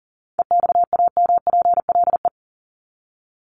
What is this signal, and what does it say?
Morse code, heard as a keyed tone, about 40 wpm, 700 Hz. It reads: EXAMPLE
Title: Top 500 Words 40wpm